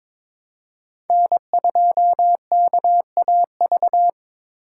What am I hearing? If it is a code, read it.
Morse code, heard as a keyed tone, about 22 wpm, 700 Hz. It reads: N2KAV
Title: USA Call Signs - 5 Characters - 8x Character Spacing - 22wpm